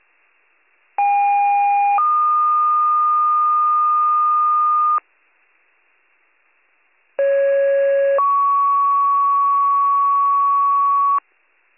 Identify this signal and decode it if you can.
Two-tone page, heard as a one-second tone, then a three-second tone, about 810 then 1220 Hz, about 550 then 1120 Hz. It is heard from 1.0 to 11.2 s.